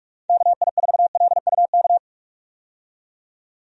Morse code, heard as a keyed tone, about 46 wpm, 700 Hz. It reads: KI4LUK
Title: USA Call Signs - 6 Characters - 45wpm